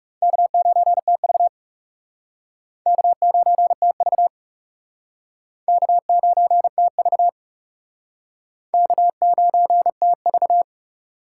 Morse code, heard as a keyed tone, about 36 wpm, 700 Hz. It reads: K9TV K9TV K9TV K9TV
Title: Speed Racing - USA 1x2 Call Signs 30wpm